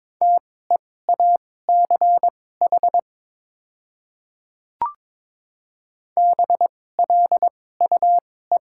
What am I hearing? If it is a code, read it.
Morse code, heard as a keyed tone, about 22 wpm, spaced as for 16 wpm, 700 Hz. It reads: TEACH BLUE
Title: 2x Character Spacing - Top 500 Words - 22wpm